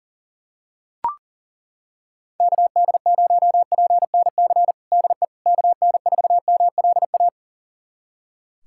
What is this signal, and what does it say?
Morse code, heard as a keyed tone, about 40 wpm, 700 Hz. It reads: KD0PNC DE KN4MLA
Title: USA Callsign DE Callsign - 6 Characters - 40wpm